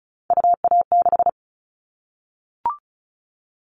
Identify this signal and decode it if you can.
Morse code, heard as a keyed tone, about 34 wpm, 700 Hz. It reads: UA6
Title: Callsign Prefixes for Top 50 DXCC - 35wpm